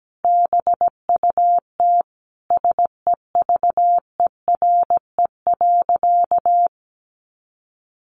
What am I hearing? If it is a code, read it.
Morse code, heard as a keyed tone, about 17 wpm, 700 Hz. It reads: BUT SEVERE.
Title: Sentences from Top 1400 Words 17wpm